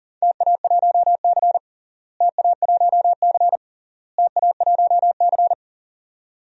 Morse code, heard as a keyed tone, about 40 wpm, 700 Hz. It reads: TA1C TA1C TA1C
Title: DXCC-100 Call Signs - 4 Characters - Repeated 3x - 40wpm